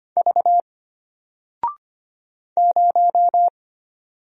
Morse code, heard as a keyed tone, about 25 wpm, 700 Hz. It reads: V 0
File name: Single-Letter-Number-25wpm